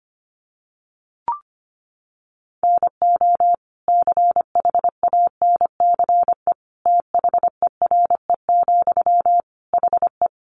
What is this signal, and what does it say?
Morse code, heard as a keyed tone, about 25 wpm, 700 Hz. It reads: NO CHANCE THERE, HE
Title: Sentences from Top 800 Words 25wpm